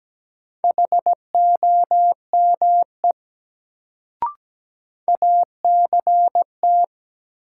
Morse code, heard as a keyed tone, about 17 wpm, 700 Hz. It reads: HOME ACT
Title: Sentences from Top 200 Words - New Word Drill - 17wpm